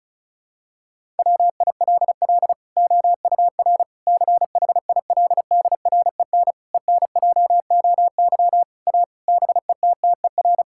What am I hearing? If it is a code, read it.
Morse code, heard as a keyed tone, about 34 wpm, 700 Hz. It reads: WILL OUR CHILDREN ENJOY A BETTER
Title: Sentences from Top 1000 Words - Review Part 5 - 35wpm